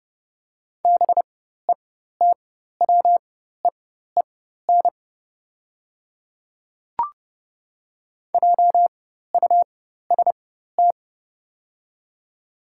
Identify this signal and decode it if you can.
Morse code, heard as a keyed tone, about 31 wpm, spaced as for 14 wpm, 700 Hz. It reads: BETWEEN JUST
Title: 4x Character Spacing - Top 500 Words - 30wpm